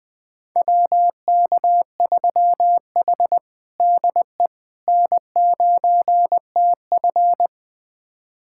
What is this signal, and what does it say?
Morse code, heard as a keyed tone, about 20 wpm, 700 Hz. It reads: WK3H DE N9TF
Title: USA Callsign DE Callsign - 4 Characters - 20wpm